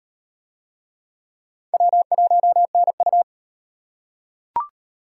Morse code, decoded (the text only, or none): W1NU